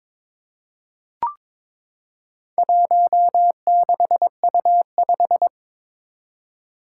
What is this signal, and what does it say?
Morse code, heard as a keyed tone, about 22 wpm, 700 Hz. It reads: 16U5
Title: Four Challenging Letter-Number Combinations 22wpm